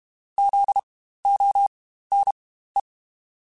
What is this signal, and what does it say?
Morse code, heard as a keyed tone, about 32 wpm, spaced as for 15 wpm, 800 Hz. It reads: ZONE
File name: Spaced Alphas / Spaced Alphas mp3s / Zs sp @30